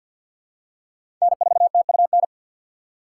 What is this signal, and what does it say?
Morse code, heard as a keyed tone, about 52 wpm, 700 Hz. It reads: N4TUN
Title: USA Call Signs - 5 Characters - 4x Character Spacing - 50wpm